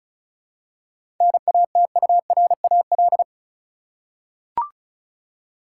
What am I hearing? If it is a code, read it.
Morse code, heard as a keyed tone, about 36 wpm, 700 Hz. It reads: NATURAL
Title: Seven Letter Words - 35wpm